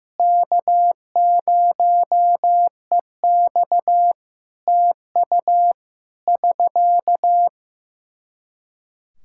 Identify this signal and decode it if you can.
Morse code, heard as a keyed tone, about 15 wpm, 700 Hz. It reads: K0EX TU <SK>
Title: Practice QSO - Exchange 4 - Ending - 15wpm